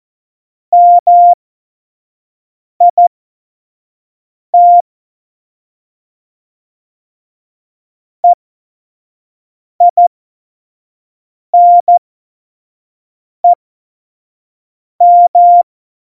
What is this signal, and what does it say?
Morse code, heard as a keyed tone, about 14 wpm, spaced as for 5 wpm, 700 Hz. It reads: MIT EINEM
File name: Morsekurs / Witze / Witz 30. April 2019, 25 F70